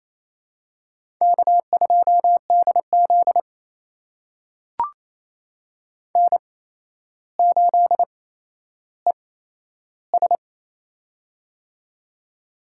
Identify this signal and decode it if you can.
Morse code, heard as a keyed tone, about 28 wpm, spaced as for 12 wpm, 700 Hz. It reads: K2DZ N8ES
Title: USA Call Signs - 4 Characters - 8x Character Spacing - 28wpm